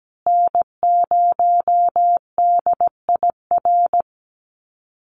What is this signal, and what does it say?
Morse code, heard as a keyed tone, about 17 wpm, 700 Hz. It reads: N0DIR
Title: Call Signs Repeated 2x - 17wpm